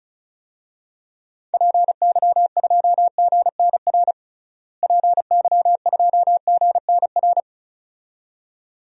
Morse code, heard as a keyed tone, about 34 wpm, 700 Hz. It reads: PY2GNR PY2GNR
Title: DXCC-100 Call Signs - 6 Characters - Repeated 2x - 35wpm